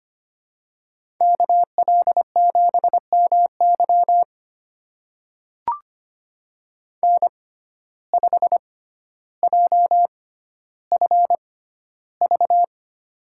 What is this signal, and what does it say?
Morse code, heard as a keyed tone, about 25 wpm, spaced as for 13 wpm, 700 Hz. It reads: KL7MY N5JFV
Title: USA Call Signs - 5 Characters - 6x Character Spacing - 25wpm